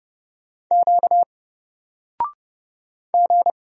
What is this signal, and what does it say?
Morse code, heard as a keyed tone, about 30 wpm, 700 Hz. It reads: Q G
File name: Single-Letter-Number-30wpm